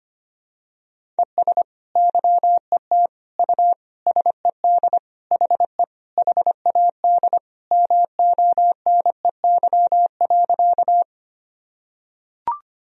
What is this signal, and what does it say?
Morse code, heard as a keyed tone, about 25 wpm, 700 Hz. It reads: ES YET U SED HE HAD MONEY.
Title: Sentences from Top 500 Words and Abbreviations 25wpm